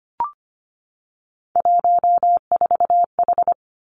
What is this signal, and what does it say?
Morse code, heard as a keyed tone, about 25 wpm, 700 Hz. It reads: 14H
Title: Three Challenging Letter-Number Combinations 25wpm